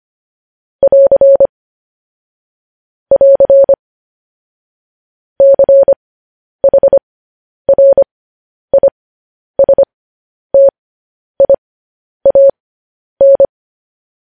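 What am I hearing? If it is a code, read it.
Morse code, heard as a keyed tone, about 25 wpm, spaced as for 10 wpm, 550 Hz. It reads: <AR> <AR> CHRISTIAN
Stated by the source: Morse news headlines at 10.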